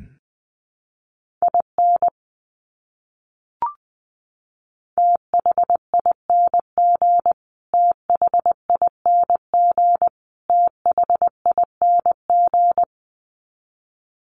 Morse code, heard as a keyed tone, about 20 wpm, 700 Hz. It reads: IN THING THING THING
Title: Top 200 Words Repeated 20wpm